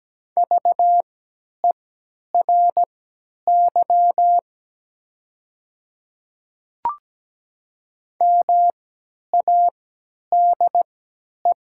Morse code, heard as a keyed tone, about 17 wpm, spaced as for 10 wpm, 700 Hz. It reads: VERY MADE
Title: 3x Character Spacing - Top 300 Words - 17wpm